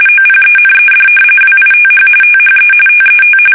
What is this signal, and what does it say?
RTTY, about 45.45 baud. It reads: OPENED WITH THE TET
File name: fsk850 / battleshipnj